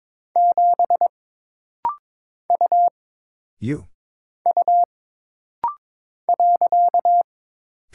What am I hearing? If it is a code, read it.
Morse code, heard as a keyed tone, about 22 wpm, 700 Hz. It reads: 7 U U .
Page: This rapid-fire timing is geared to push your ICR (Instant Character Recognition) skills.